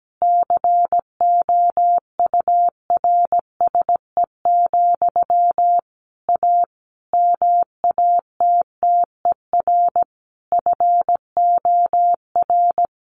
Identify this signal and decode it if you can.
Morse code, heard as a keyed tone, about 17 wpm, 700 Hz. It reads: COURSE, A MATTER FOR
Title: Sentences from Top 1500 Words 17wpm